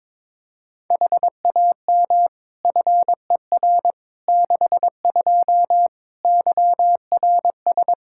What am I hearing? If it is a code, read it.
Morse code, heard as a keyed tone, about 21 wpm, 700 Hz. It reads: HAM FER 62 YRS
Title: Practice QSO - Exchange 3 - More Info - 22wpm